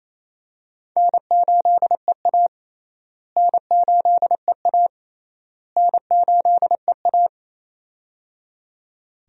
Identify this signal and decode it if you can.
Morse code, heard as a keyed tone, about 28 wpm, 700 Hz. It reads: N8EA N8EA N8EA
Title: USA Call Signs - 4 Characters - Repeated 3x - 28wpm